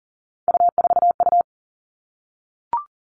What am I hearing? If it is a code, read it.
Morse code, heard as a keyed tone, about 38 wpm, 700 Hz. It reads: U4U
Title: Three Challenging Letter-Number Combinations 40wpm